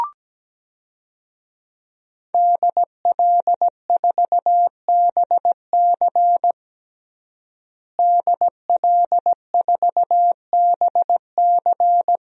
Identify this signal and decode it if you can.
Morse code, heard as a keyed tone, about 17 wpm, 700 Hz. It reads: DL4BC DL4BC
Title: DXCC-100 Call Signs - 5 Characters - Repeated 2x - 17wpm